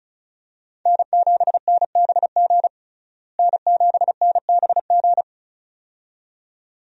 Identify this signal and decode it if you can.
Morse code, heard as a keyed tone, about 36 wpm, 700 Hz. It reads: N7NBG N7NBG
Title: USA Call Signs - 5 Characters - Repeated 2x - 35wpm